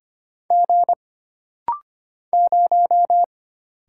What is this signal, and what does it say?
Morse code, heard as a keyed tone, about 25 wpm, 700 Hz. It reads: G 0
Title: Single Letter-Number - Rapid-Fire 25wpm